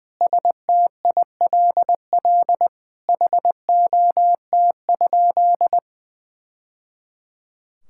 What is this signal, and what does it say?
Morse code, heard as a keyed tone, about 20 wpm, 700 Hz. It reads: STILL HOT?
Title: Sentences from Top 1000 Words - Review Part 6 - 20wpm